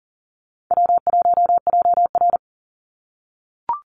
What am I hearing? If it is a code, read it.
Morse code, heard as a keyed tone, about 40 wpm, 700 Hz. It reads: W1JR
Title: Easy Call Signs - 40wpm